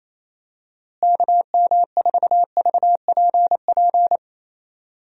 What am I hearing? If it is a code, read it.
Morse code, heard as a keyed tone, about 28 wpm, 700 Hz. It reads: KM4VPP
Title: USA Call Signs - 6 Characters - Repeated 3x - 28wpm